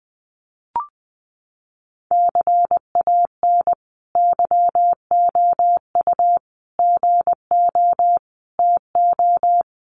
Morse code, heard as a keyed tone, about 20 wpm, 700 Hz. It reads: CAN YOU GO TO
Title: Sentences from Top 1500 Words 20wpm